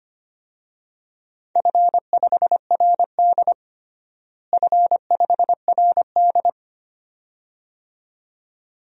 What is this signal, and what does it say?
Morse code, heard as a keyed tone, about 25 wpm, 700 Hz. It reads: F5RD F5RD
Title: DXCC-100 Call Signs - 4 Characters - Repeated 2x - 25wpm